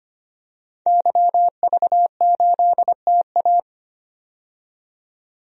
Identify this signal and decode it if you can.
Morse code, heard as a keyed tone, about 25 wpm, 700 Hz. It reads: YV8TA
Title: DXCC-100 Call Signs - 5 Characters - 25wpm